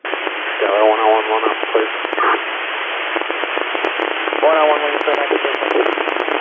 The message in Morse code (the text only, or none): none